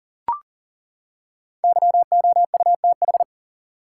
Morse code, heard as a keyed tone, about 40 wpm, 700 Hz. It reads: YOUTH
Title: Five Letter Words - 40wpm